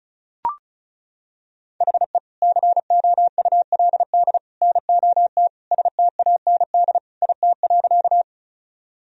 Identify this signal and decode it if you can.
Morse code, heard as a keyed tone, about 35 wpm, 700 Hz. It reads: HE COULD NOT STAND IT.
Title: Sentences from Top 1000 Words - Review Part 1 - 35wpm